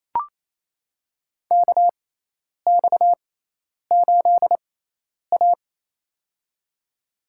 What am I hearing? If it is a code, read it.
Morse code, heard as a keyed tone, about 28 wpm, spaced as for 10 wpm, 700 Hz. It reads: KX8A
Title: USA Call Signs - 4 Characters - 6x Character Spacing - 28wpm